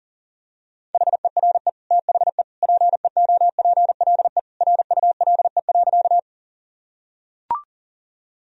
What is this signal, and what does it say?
Morse code, heard as a keyed tone, about 37 wpm, 700 Hz. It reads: HERE THE PEOPLE RULE.
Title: Sentences from Top 1000 Words - Review Part 2 - 40wpm